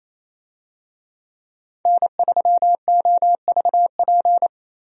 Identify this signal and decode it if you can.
Morse code, heard as a keyed tone, about 28 wpm, 700 Hz. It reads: N3OVP
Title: USA Call Signs - 5 Characters - 28wpm